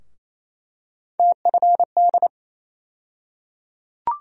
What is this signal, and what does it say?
Morse code, heard as a keyed tone, about 28 wpm, 700 Hz. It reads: TFD
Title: USA Call Sign Postfixes - 3 Letters 28wpm